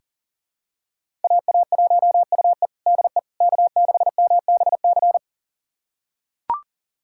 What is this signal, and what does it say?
Morse code, heard as a keyed tone, about 40 wpm, 700 Hz. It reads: AA1UE DE K6MBC
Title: USA Callsign DE Callsign - 5 Characters - 40wpm